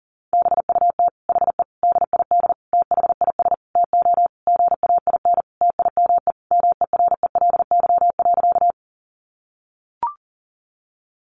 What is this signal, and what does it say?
Morse code, heard as a keyed tone, about 37 wpm, 700 Hz. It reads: BUT HE DID THIS TO GAIN TIME MERELY.
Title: Sentences from Top 1400 Words 40wpm